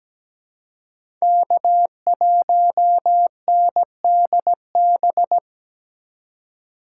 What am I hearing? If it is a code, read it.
Morse code, heard as a keyed tone, about 17 wpm, 700 Hz. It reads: K1NDB
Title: USA Call Signs - 5 Characters - 4x Character Spacing - 17wpm